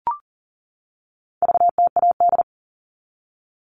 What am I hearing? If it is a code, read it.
Morse code, heard as a keyed tone, about 40 wpm, 700 Hz. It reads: VTAD
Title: All Cut Numbers - 4-Digit 40wpm